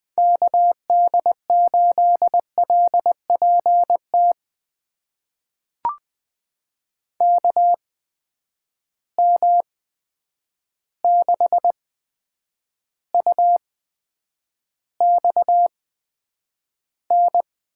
Morse code, heard as a keyed tone, about 21 wpm, spaced as for 9 wpm, 700 Hz. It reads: KD8LPT KM6UXN
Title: USA Call Signs - 6 Characters - 8x Character Spacing - 20wpm